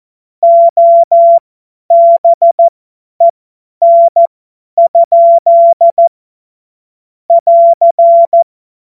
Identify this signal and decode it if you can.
Morse code, heard as a keyed tone, about 14 wpm, spaced as for 10 wpm, 680 Hz. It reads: OBEN? <AR>
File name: Morsekurs / Witze / Witz 18. Januar 2022, 50 F70